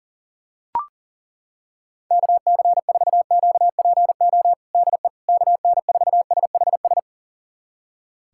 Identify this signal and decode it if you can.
Morse code, heard as a keyed tone, about 40 wpm, 700 Hz. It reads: KC4QPO DE KN4SHS
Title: USA Callsign DE Callsign 40wpm